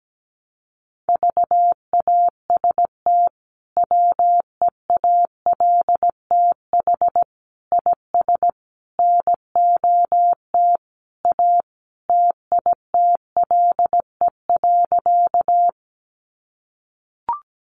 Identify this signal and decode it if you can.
Morse code, heard as a keyed tone, about 17 wpm, 700 Hz. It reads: VAST WEALTH IS NOT A TITLE.
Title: Sentences from Top 1300 Words 17wpm